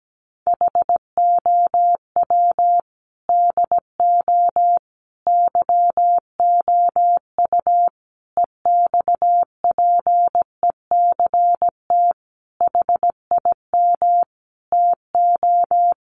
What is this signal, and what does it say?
Morse code, heard as a keyed tone, about 17 wpm, 700 Hz. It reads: HOW DO YOU EXPECT HIM TO
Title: Sentences from Top 1100 Words 17wpm